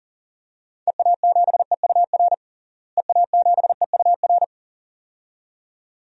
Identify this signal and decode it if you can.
Morse code, heard as a keyed tone, about 37 wpm, 700 Hz. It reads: EA7EUR EA7EUR
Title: DXCC-100 Call Signs - 6 Characters - Repeated 2x - 40wpm